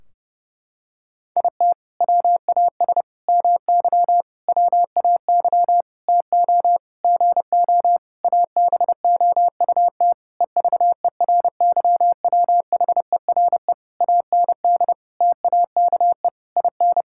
Morse code, heard as a keyed tone, about 32 wpm, 700 Hz. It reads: IT WAS MY WAY TO GO ABOUT EVERYWHERE AND TAKE IN
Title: Sentences from Top 1600 Words 30wpm